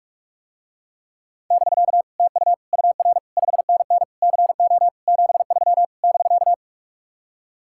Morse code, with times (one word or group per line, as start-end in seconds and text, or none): 1.5-2.0 <BK>
2.2-2.5 TU
2.7-3.2 UR
3.4-4.0 5NN
4.2-4.9 CO
5.1-5.8 73
6.0-6.5 <BK>